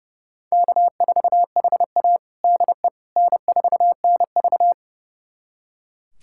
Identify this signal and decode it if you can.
Morse code, heard as a keyed tone, about 30 wpm, 700 Hz. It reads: K4HA DE N4NV
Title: USA Callsign DE Callsign - 4 Characters - 30wpm